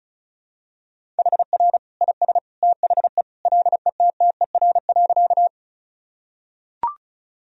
Morse code, heard as a keyed tone, about 36 wpm, 700 Hz. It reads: HR IS THE LETTER.
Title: Sentences from Top 300 Words and Abbreviations 35wpm